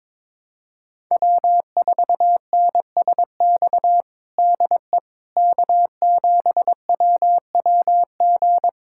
Morse code, heard as a keyed tone, about 22 wpm, 700 Hz. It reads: W4NSX DE K7WWG
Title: USA Callsign DE Callsign 22wpm